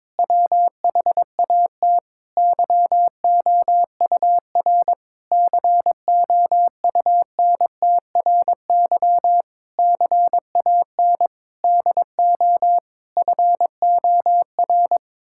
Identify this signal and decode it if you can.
Morse code, heard as a keyed tone, about 22 wpm, 700 Hz. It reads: WHAT YOUR COUNTRY CAN DO FOR
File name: practice / 22wpm / Sets of 8 Words - 22wpm